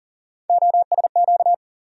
Morse code, heard as a keyed tone, about 40 wpm, 700 Hz. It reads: OSQ
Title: USA Call Sign Postfixes - 3 Letters 40wpm